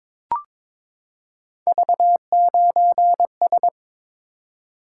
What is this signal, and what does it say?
Morse code, heard as a keyed tone, about 22 wpm, 700 Hz. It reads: V9S
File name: practice / 22wpm / Three Letter-Number Combinations - 22wpm